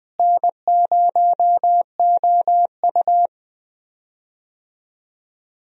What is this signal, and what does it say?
Morse code, heard as a keyed tone, about 20 wpm, 700 Hz. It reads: N0OU
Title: USA Call Signs - 4 Characters - Repeated 3x - 20wpm